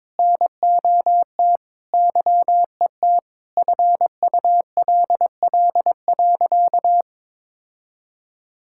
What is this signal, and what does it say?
Morse code, heard as a keyed tone, about 21 wpm, 700 Hz. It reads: NOT YET FULL.
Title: Sentences from Top 1600 Words 22wpm